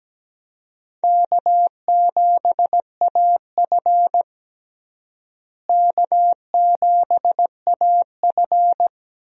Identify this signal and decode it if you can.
Morse code, heard as a keyed tone, about 17 wpm, 700 Hz. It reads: K7AF K7AF
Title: USA Call Signs - 4 Characters - Repeated 3x - 17wpm